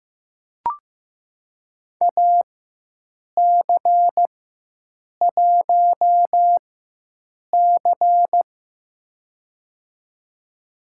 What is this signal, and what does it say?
Morse code, heard as a keyed tone, about 15 wpm, spaced as for 7 wpm, 700 Hz. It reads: AC1C
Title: USA Call Signs - 4 Characters - 4x Character Spacing - 15wpm